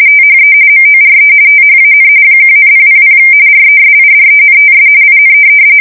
RTTY, about 45.45 baud. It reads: THE MORE THAN 1,000 RESIDENTS IN
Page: 170 Shift 2125/2295